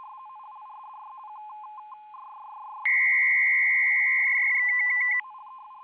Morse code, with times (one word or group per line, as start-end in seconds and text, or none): none